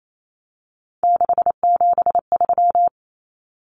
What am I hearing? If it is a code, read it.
Morse code, heard as a keyed tone, about 28 wpm, 700 Hz. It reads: 673
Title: Common Cut Numbers - 3-Digit 28wpm